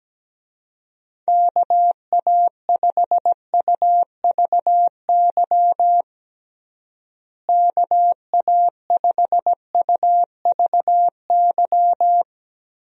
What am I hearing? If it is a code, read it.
Morse code, heard as a keyed tone, about 17 wpm, 700 Hz. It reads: KA5UVY KA5UVY
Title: USA Call Signs - 6 Characters - Repeated 2x - 17wpm